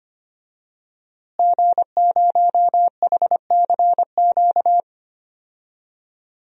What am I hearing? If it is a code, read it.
Morse code, heard as a keyed tone, about 25 wpm, 700 Hz. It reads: G0HCQ
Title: DXCC-100 Call Signs - 5 Characters - Repeated 3x - 25wpm